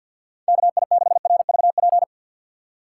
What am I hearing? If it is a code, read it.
Morse code, heard as a keyed tone, about 51 wpm, 700 Hz. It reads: KI6RVP
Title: USA Call Signs - 6 Characters - 6x Character Spacing - 50wpm